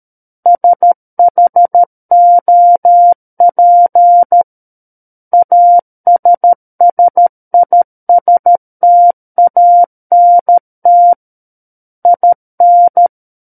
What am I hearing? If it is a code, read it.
Morse code, heard as a keyed tone, about 13 wpm, 700 Hz. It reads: SHOP ASSISTANT IN
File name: Book One / 04-Chapter 4 - 13 WPM